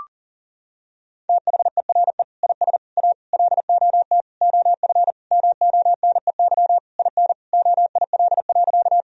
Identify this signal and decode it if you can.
Morse code, heard as a keyed tone, about 40 wpm, 700 Hz. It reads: THERE IS A LOT OF MONEY IN OIL.
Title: Sentences from Top 1000 Words - Review Part 4 - 40wpm